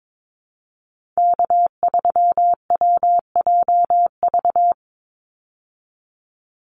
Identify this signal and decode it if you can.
Morse code, heard as a keyed tone, about 22 wpm, 700 Hz. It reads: K3WJV
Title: Call Signs Repeated 3x - 22wpm